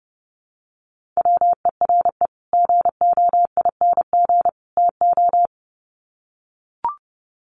Morse code, heard as a keyed tone, about 28 wpm, 700 Hz. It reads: WERE GOING TO
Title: Sets of 3 Words 30wpm